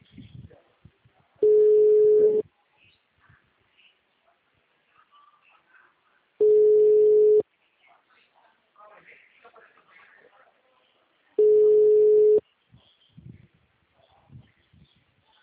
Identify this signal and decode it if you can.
Ringback tone (Europe, 425 Hz), with speech.